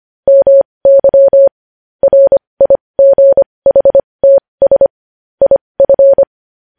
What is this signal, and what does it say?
Morse code, heard as a keyed tone, about 25 wpm, spaced as for 20 wpm, 550 Hz. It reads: MY RIGHTS IF
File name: podcast / News headlines-1x-25x20